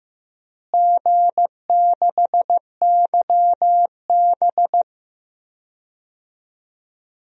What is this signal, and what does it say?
Morse code, heard as a keyed tone, about 15 wpm, 700 Hz. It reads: G6YB
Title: DXCC-100 Call Signs - 4 Characters - Repeated 3x - 15wpm